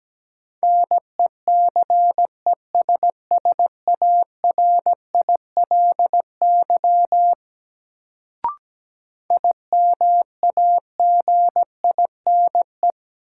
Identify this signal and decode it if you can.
Morse code, heard as a keyed tone, about 17 wpm, 700 Hz. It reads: NECESSARILY IMAGINE
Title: Sentences from Top 1400 Words - New Word Drill - 17wpm